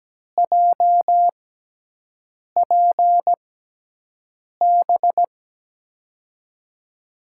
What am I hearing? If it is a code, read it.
Morse code, heard as a keyed tone, about 17 wpm, spaced as for 6 wpm, 700 Hz. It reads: JPB
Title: USA Call Signs - 6 Characters - 6x Character Spacing - 17wpm